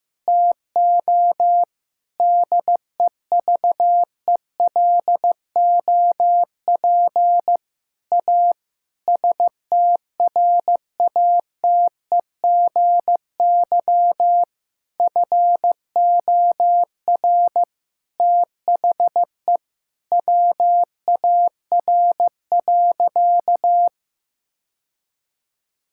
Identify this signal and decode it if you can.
Morse code, heard as a keyed tone, about 15 wpm, 700 Hz. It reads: TO DEVELOP A STRATEGY FOR THE WAR.
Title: Sentences from Top 1000 Words - Review Part 5 - 15wpm